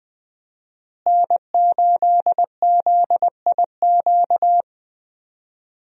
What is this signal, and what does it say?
Morse code, heard as a keyed tone, about 21 wpm, 700 Hz. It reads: N8ZIQ
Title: USA Call Signs - 5 Characters - Repeated 2x - 20wpm